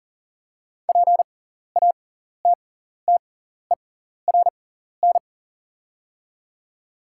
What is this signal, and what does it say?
Morse code, heard as a keyed tone, about 40 wpm, spaced as for 14 wpm, 690 Hz. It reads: PATTERN
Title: 6x Character Spacing - Top 500 Words - 40wpm